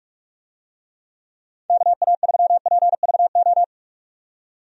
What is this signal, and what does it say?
Morse code, heard as a keyed tone, about 45 wpm, 700 Hz. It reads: KA3PVO
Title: USA Call Signs - 6 Characters - 45wpm